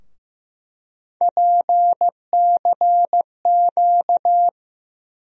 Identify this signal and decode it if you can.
Morse code, heard as a keyed tone, about 15 wpm, 700 Hz. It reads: PCQ
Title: USA Call Sign Postfixes - 3 Letters 15wpm